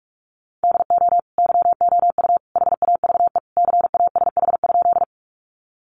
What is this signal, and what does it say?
Morse code, heard as a keyed tone, about 44 wpm, 700 Hz. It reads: DO YOU HAVE CASH?
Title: Sentences from Top 700 Words 45wpm